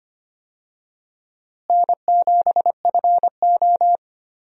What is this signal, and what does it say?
Morse code, heard as a keyed tone, about 25 wpm, 700 Hz. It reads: N7FO
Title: USA Call Signs - 4 Characters - Repeated 2x - 25wpm